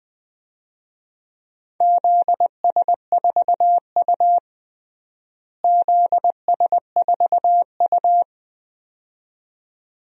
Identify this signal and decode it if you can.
Morse code, heard as a keyed tone, about 20 wpm, 700 Hz. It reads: ZS4U ZS4U
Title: DXCC-100 Call Signs - 4 Characters - Repeated 2x - 20wpm